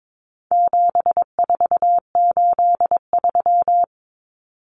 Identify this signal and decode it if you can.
Morse code, heard as a keyed tone, about 22 wpm, 700 Hz. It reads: 7483
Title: Common Cut Numbers - 4-Digit 22wpm